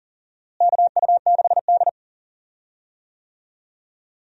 Morse code, heard as a keyed tone, about 40 wpm, 700 Hz. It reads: KU6D
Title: USA Call Signs - 4 Characters - Repeated 3x - 40wpm